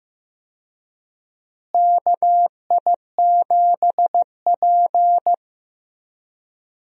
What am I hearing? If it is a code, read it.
Morse code, heard as a keyed tone, about 15 wpm, 700 Hz. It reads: KI7P
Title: USA Call Signs - 4 Characters - Repeated 2x - 15wpm